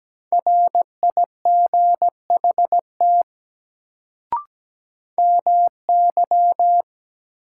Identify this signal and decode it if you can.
Morse code, heard as a keyed tone, about 17 wpm, 700 Hz. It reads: RIGHT MY
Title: Sentences from Top 200 Words - New Word Drill - 17wpm